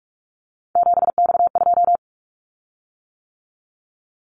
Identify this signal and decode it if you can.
Morse code, heard as a keyed tone, about 44 wpm, 700 Hz. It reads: 7X2
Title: Callsign Prefixes for Top 100 DXCC - 45wpm